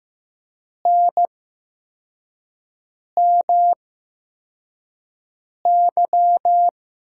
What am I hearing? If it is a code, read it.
Morse code, heard as a keyed tone, about 15 wpm, spaced as for 4 wpm, 700 Hz. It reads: NMY
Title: USA Call Signs - 5 Characters - 8x Character Spacing - 15wpm